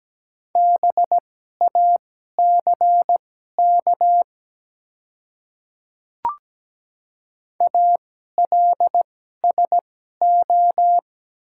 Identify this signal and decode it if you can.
Morse code, heard as a keyed tone, about 17 wpm, spaced as for 12 wpm, 700 Hz. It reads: BACK ALSO
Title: 2x Character Spacing - Top 200 Words - 17wpm